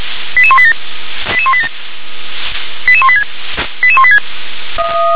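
Five-tone selective call (ZVEI), heard as quick runs of five tones, tones of about 70 ms. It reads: 88187 88187 88186 88186
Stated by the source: ZVEI five-tone selective call received on SDR.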